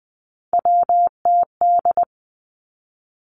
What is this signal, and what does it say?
Morse code, heard as a keyed tone, about 20 wpm, 700 Hz. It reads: WTD
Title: Three Challenging Letter-Number Combinations 20wpm